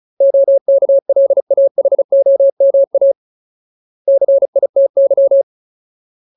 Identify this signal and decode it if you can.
Morse code, heard as a keyed tone, about 34 wpm, 550 Hz. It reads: OKLAHOMA CITY
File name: Ultimate Word List / 35WPM Ultimate